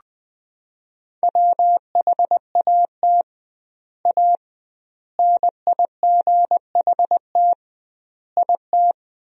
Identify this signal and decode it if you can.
Morse code, heard as a keyed tone, about 20 wpm, 700 Hz. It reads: WHAT A NIGHT IT
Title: 2x Word-Spacing - Sentences from Top 300 Words - 20wpm